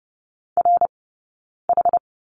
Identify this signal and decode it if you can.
Morse code, heard as a keyed tone, about 30 wpm, spaced as for 9 wpm, 690 Hz. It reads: RH
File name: Lesson 19d - Letter w - Two Characters 30wpm